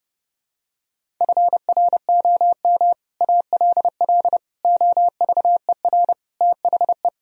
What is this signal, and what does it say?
Morse code, heard as a keyed tone, about 30 wpm, 700 Hz. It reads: FROM ALL OVER THE